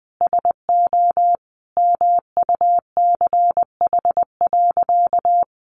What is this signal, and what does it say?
Morse code, heard as a keyed tone, about 20 wpm, 700 Hz. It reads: SO MUCH.
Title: Sentences from Top 1500 Words 20wpm